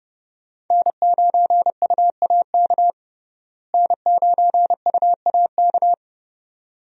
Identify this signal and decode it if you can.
Morse code, heard as a keyed tone, about 30 wpm, 700 Hz. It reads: N9UAK N9UAK
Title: USA Call Signs - 5 Characters - Repeated 2x - 30wpm